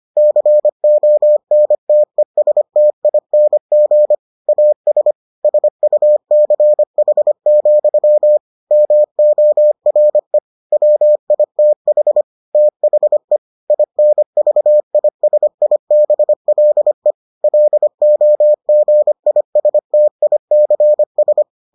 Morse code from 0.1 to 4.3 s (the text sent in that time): CONTESTING